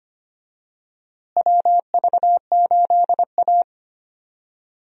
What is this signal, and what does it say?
Morse code, heard as a keyed tone, about 25 wpm, 700 Hz. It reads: WV8A
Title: USA Call Signs - 4 Characters - Repeated 3x - 25wpm